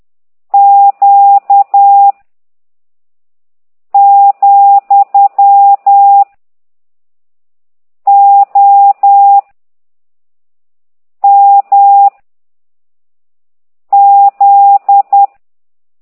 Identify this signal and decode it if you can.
Morse code, heard as a keyed tone, about 10 wpm, 800 Hz. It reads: Q , O M Z